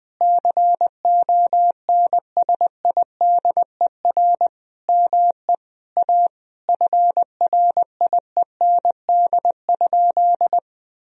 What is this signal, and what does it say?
Morse code, heard as a keyed tone, about 20 wpm, 700 Hz. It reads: CONSIDER ME A FRIEND?
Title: Sentences from Top 1000 Words - Review Part 4 - 20wpm